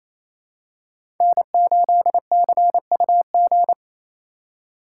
Morse code, heard as a keyed tone, about 28 wpm, 700 Hz. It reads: N8CUG
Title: USA Call Signs - 4x Character Spacing - 28wpm